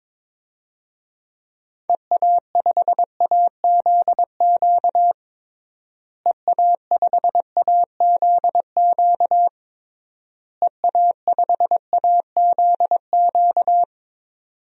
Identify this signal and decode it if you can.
Morse code, heard as a keyed tone, about 22 wpm, 700 Hz. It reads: EA5AZQ EA5AZQ EA5AZQ
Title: DXCC-100 Call Signs - 6 Characters - Repeated 3x - 22wpm